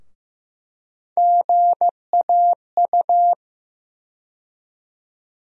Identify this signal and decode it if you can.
Morse code, heard as a keyed tone, about 16 wpm, 700 Hz. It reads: GAU
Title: USA Call Sign Postfixes - 3 Letters 15wpm